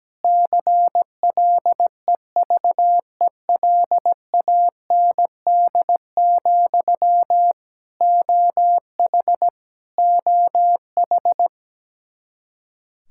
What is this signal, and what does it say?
Morse code, heard as a keyed tone, about 17 wpm, 700 Hz. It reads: CLEVELAND, OH OH
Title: Practice QSO - Exchange 1 - Beginning - 17wpm